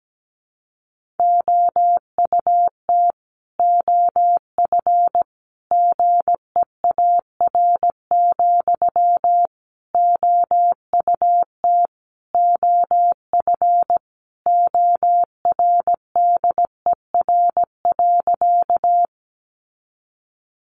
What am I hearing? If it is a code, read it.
Morse code, heard as a keyed tone, about 17 wpm, 700 Hz. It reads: OUT OF GEAR, OUT OF ORDER.
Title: Sentences from Top 1400 Words 17wpm